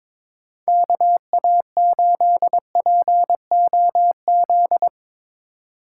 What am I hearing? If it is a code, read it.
Morse code, heard as a keyed tone, about 22 wpm, 700 Hz. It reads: KA8POZ
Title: K1USN SST QSO Practice - 22wpm